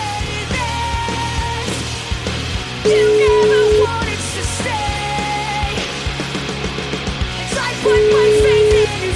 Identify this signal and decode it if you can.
Ringback tone (Europe, 425 Hz), with noise.